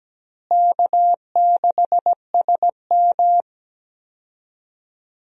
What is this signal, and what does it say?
Morse code, heard as a keyed tone, about 17 wpm, 700 Hz. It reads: K6SM
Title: USA Call Signs - 4 Characters - Repeated 2x - 17wpm